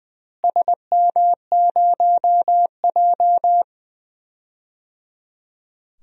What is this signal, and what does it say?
Morse code, heard as a keyed tone, about 20 wpm, 700 Hz. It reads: SM0J
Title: DXCC-100 Call Signs - 4 Characters - Repeated 3x - 20wpm